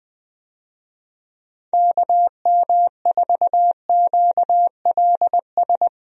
Morse code, heard as a keyed tone, about 20 wpm, 700 Hz. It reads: KM4QLS
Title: USA Call Signs - 6 Characters - Repeated 2x - 20wpm